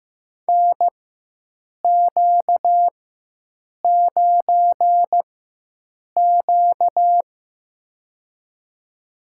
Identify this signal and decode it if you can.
Morse code, heard as a keyed tone, about 15 wpm, spaced as for 7 wpm, 700 Hz. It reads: NQ9Q
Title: USA Call Signs - 4 Characters - 4x Character Spacing - 15wpm